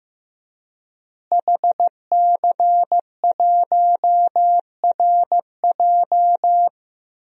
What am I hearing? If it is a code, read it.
Morse code, heard as a keyed tone, about 15 wpm, 700 Hz. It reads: HC1RJ
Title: DXCC-100 Call Signs - 5 Characters - Repeated 3x - 15wpm